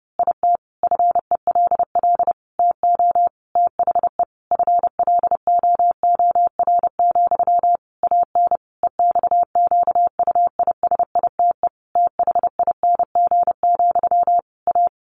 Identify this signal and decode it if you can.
Morse code, heard as a keyed tone, about 30 wpm, 700 Hz. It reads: IT FELL TO THE FLOOR, AN EXQUISITE THING, A
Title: A Sound Of Thunder 30wpm